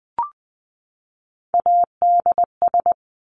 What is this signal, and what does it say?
Morse code, heard as a keyed tone, about 20 wpm, 700 Hz. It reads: ADS
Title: Abbreviations 20wpm